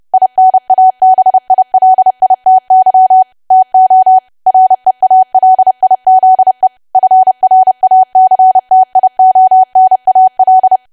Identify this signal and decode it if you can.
Morse code, heard as a keyed tone, about 30 wpm, 750 Hz. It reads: INABILITY TO REALIZE FRACTIONAL
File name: cwa / W1AW-Level3 / 30WPM